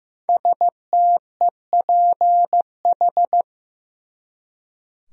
Morse code, heard as a keyed tone, about 15 wpm, 700 Hz. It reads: STEPH
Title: USA Common Names - 15wpm